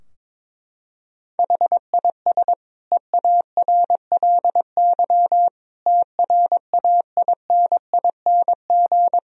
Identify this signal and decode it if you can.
Morse code, heard as a keyed tone, about 22 wpm, 700 Hz. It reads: HIS EARLY TRAINING
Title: Sentences from Top 1000 Words - Review Part 4 - 22wpm